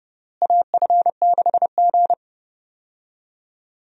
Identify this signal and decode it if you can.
Morse code, heard as a keyed tone, about 30 wpm, 700 Hz. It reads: AF6G
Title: USA Call Signs - 4 Characters - Repeated 2x - 30wpm